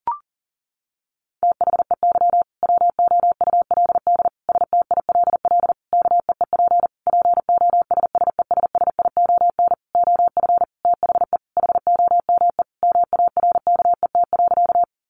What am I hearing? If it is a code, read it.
Morse code, heard as a keyed tone, about 40 wpm, 700 Hz. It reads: THEY WOULD STILL KEEP POSSESSION OF THE HOME MARKET.
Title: Sentences from Top 1400 Words 40wpm